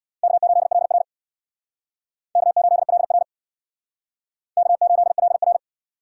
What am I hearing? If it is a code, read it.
Morse code, heard as a keyed tone, about 64 wpm, 700 Hz. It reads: K8LR K8LR K8LR